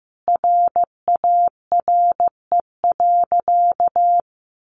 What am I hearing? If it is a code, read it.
Morse code, heard as a keyed tone, about 15 wpm, 700 Hz. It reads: RARE.
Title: Sentences from Top 1100 Words 15wpm